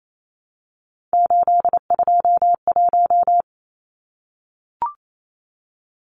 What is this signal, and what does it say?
Morse code, heard as a keyed tone, about 28 wpm, 700 Hz. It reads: 821
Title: Numbers - 3-Digit 28wpm